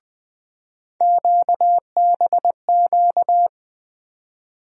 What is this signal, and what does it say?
Morse code, heard as a keyed tone, about 20 wpm, 700 Hz. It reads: QBQ
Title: USA Call Sign Postfixes - 3 Letters 20wpm